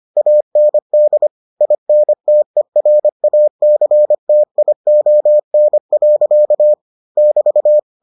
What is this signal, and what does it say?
Morse code, heard as a keyed tone, about 25 wpm, 600 Hz. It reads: AND INTERACTION. <BT>